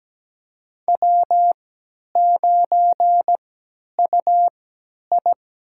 Morse code, heard as a keyed tone, about 17 wpm, spaced as for 10 wpm, 700 Hz. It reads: W9UI
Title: USA Call Signs - 4 Characters - 3x Character Spacing - 17wpm